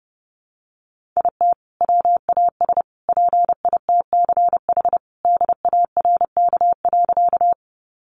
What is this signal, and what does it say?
Morse code, heard as a keyed tone, about 30 wpm, 700 Hz. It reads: IT WAS PITCH DARK.
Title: Sentences from Top 1300 Words 30wpm